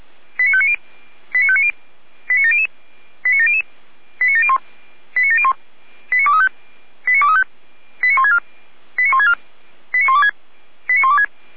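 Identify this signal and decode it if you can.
Five-tone selective call (ZVEI), heard as quick runs of five tones, tones of about 70 ms. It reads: 89590 89590 89700 89700 89701 89701 89235 89235 89164 89164 89126 89126
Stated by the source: Received by SDR as ZVEI five-tone selective call.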